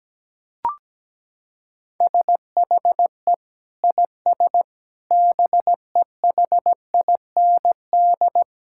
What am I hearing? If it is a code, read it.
Morse code, heard as a keyed tone, about 17 wpm, 700 Hz. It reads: SHE IS BEHIND
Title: Sentences from Top 1000 Words - Review Part 6 - 17wpm